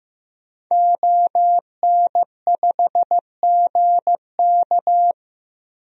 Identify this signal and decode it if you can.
Morse code, heard as a keyed tone, about 16 wpm, 700 Hz. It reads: ON5GK
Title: DXCC-100 Call Signs - 5 Characters - Repeated 3x - 15wpm